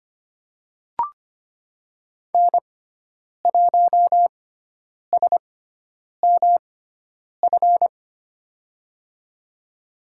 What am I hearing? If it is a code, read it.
Morse code, heard as a keyed tone, about 25 wpm, spaced as for 9 wpm, 700 Hz. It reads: N1SMF
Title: USA Call Signs - 5 Characters - 6x Character Spacing - 25wpm